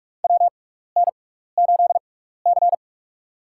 Morse code, heard as a keyed tone, about 43 wpm, spaced as for 15 wpm, 700 Hz. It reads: WN8C
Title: USA Call Signs - 6x Character Spacing - 45wpm